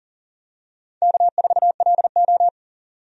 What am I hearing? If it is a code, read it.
Morse code, heard as a keyed tone, about 40 wpm, 700 Hz. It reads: K4LO
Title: USA Call Signs - 4 Characters - 40wpm